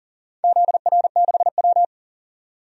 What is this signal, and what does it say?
Morse code, heard as a keyed tone, about 40 wpm, 700 Hz. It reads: ZR6W
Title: DXCC-100 Call Signs - 4 Characters - 40wpm